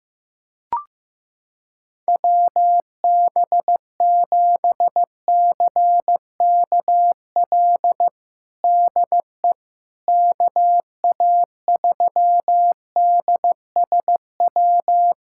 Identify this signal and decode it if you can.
Morse code, heard as a keyed tone, about 15 wpm, 700 Hz. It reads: WB7CKL DE KA3DSW
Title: USA Callsign DE Callsign - 6 Characters - 15wpm